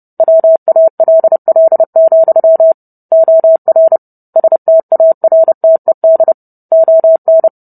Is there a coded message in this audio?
Yes: Morse code from 0.2 to 7.5 s, heard as a keyed tone, about 30 wpm, 650 Hz. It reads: WALL, OR STARTED ON